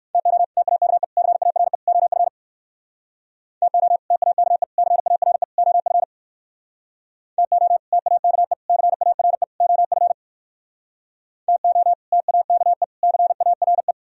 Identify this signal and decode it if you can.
Morse code, heard as a keyed tone, about 58 wpm, 700 Hz. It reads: TO TAKE CARE OF TO TAKE CARE OF TO TAKE CARE OF TO TAKE CARE